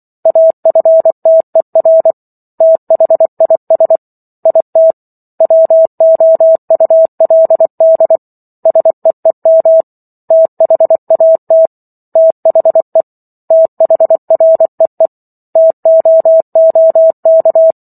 Morse code, heard as a keyed tone, about 24 wpm, 650 Hz. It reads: AFTER THIS IT WOULD SEEM THAT THE THREE TOOK
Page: War of the Worlds - 15-Chapter 15 - 24 WPM